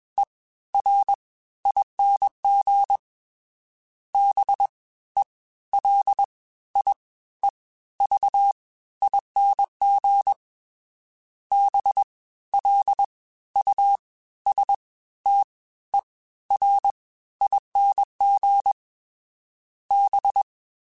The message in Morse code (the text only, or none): E R ING B E L I E V ING B L U S T E R ING B